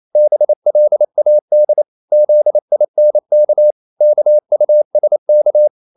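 Morse code, heard as a keyed tone, about 28 wpm, 600 Hz. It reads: BLAD ZINK KUSK
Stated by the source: Danish short words sent in Morse at 28 wpm, from CW med Gnister.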